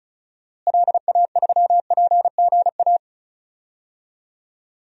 Morse code, heard as a keyed tone, about 36 wpm, 700 Hz. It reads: LA3PGA
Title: DXCC-100 Call Signs - 6 Characters - Repeated 3x - 35wpm